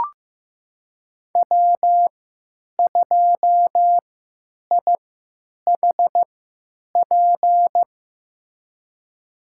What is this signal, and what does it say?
Morse code, heard as a keyed tone, about 15 wpm, spaced as for 9 wpm, 700 Hz. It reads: W2IHP
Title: USA Call Signs - 5 Characters - 3x Character Spacing - 15wpm